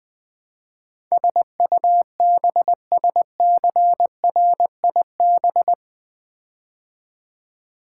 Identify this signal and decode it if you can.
Morse code, heard as a keyed tone, about 20 wpm, 700 Hz. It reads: SUBSCRIB
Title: Build A Word - 4 Letters At A Time - 10-Letter Words - 20wpm